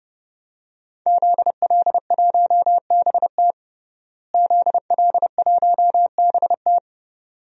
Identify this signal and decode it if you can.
Morse code, heard as a keyed tone, about 32 wpm, 700 Hz. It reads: ZL1BT ZL1BT
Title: DXCC-100 Call Signs - 5 Characters - Repeated 2x - 30wpm